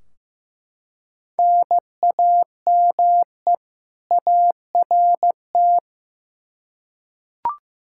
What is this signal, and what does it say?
Morse code, heard as a keyed tone, about 15 wpm, 700 Hz. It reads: NAME ART
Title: K1USN SST QSO Practice 15wpm